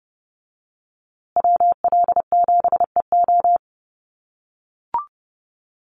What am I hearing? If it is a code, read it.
Morse code, heard as a keyed tone, about 30 wpm, 700 Hz. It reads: WL7EO
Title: USA Call Signs Repeated 3x 30wpm